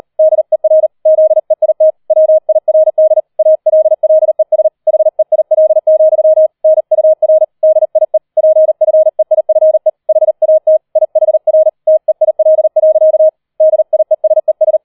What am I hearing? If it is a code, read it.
Morse code, heard as a keyed tone, about 38 wpm, 630 Hz. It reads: DER ZEIT WIRD ALLES HEIL, NUR DIE PFEIFE HAT IHR TEIL. DIESES
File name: 004 Vierter Streich 40wpm